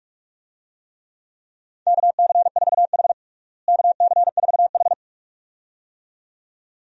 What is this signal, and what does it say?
Morse code, heard as a keyed tone, about 46 wpm, 700 Hz. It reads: KC4H KC4H
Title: USA Call Signs - 4 Characters - Repeated 2x - 45wpm